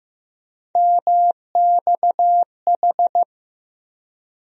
Morse code, heard as a keyed tone, about 15 wpm, 700 Hz. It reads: MXH
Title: USA Call Sign Postfixes - 3 Letters 15wpm